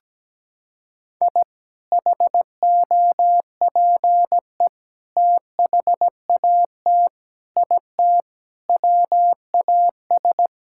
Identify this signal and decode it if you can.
Morse code, heard as a keyed tone, about 17 wpm, 700 Hz. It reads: I HOPE THAT IT WAS
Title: Sentences from Top 1000 Words - Review Part 5 - 17wpm